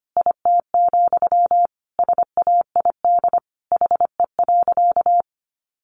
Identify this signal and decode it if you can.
Morse code, heard as a keyed tone, about 25 wpm, 700 Hz. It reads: IT, SAID HE.
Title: Sentences from Top 1100 Words 25wpm